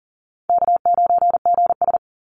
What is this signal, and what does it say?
Morse code, heard as a keyed tone, about 41 wpm, 700 Hz. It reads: K9GS
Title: Call Signs - 40wpm